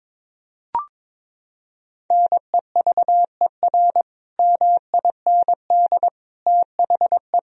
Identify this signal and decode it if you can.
Morse code, heard as a keyed tone, about 22 wpm, 700 Hz. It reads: NEVER MIND THE
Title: Sentences from Top 1000 Words - Review Part 4 - 22wpm